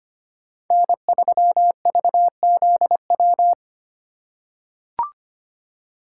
Morse code, heard as a keyed tone, about 25 wpm, 700 Hz. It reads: N3VZW
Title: USA Call Signs - 5 Characters - 4x Character Spacing - 25wpm